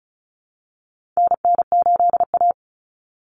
Morse code, heard as a keyed tone, about 34 wpm, 700 Hz. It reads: NN8A
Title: Common Cut Numbers - 4-Digit 35wpm